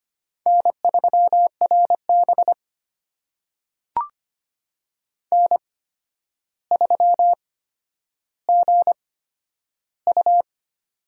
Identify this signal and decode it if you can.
Morse code, heard as a keyed tone, about 25 wpm, spaced as for 11 wpm, 700 Hz. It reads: N3RB N3GU
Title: USA Call Signs - 4 Characters - 8x Character Spacing - 25wpm